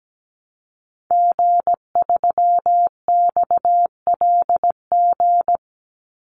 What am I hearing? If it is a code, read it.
Morse code, heard as a keyed tone, about 17 wpm, 700 Hz. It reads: G3XLG
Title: Call Signs - 17wpm